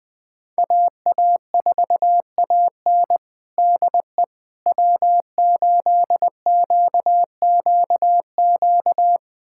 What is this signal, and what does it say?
Morse code, heard as a keyed tone, about 20 wpm, 700 Hz. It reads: AA4AN DE W8QQQ
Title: USA Callsign DE Callsign 20wpm